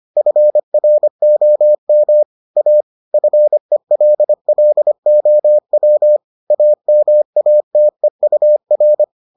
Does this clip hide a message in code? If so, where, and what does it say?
Yes: Morse code from 0.2 to 9.0 s, heard as a keyed tone, about 25 wpm, 600 Hz. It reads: FROM A FELLOW AMATEUR